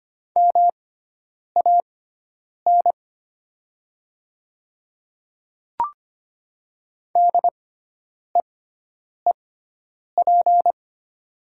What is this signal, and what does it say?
Morse code, heard as a keyed tone, about 25 wpm, spaced as for 9 wpm, 700 Hz. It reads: MAN DEEP
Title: 6x Character Spacing - Top 500 Words - 25wpm